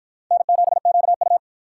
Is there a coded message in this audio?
Yes: Morse code from 0.3 to 1.4 s, heard as a keyed tone, about 53 wpm, 700 Hz. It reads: N7QU